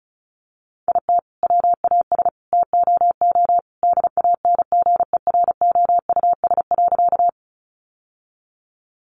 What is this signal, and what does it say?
Morse code, heard as a keyed tone, about 34 wpm, 700 Hz. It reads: IT WAS TOO DANGEROUS.
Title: Sentences from Top 1200 Words 35wpm